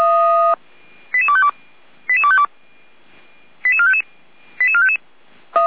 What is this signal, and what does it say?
Five-tone selective call (ZVEI), heard as quick runs of five tones, tones of about 70 ms. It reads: 88252 88252 88455 88455